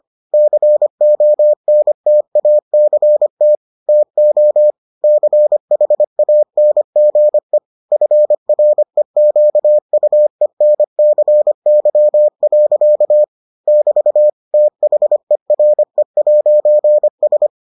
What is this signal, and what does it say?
Morse code, heard as a keyed tone, about 25 wpm, 600 Hz. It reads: CONTACT TO CHANGE FREQUENCY. <BT> THERE'S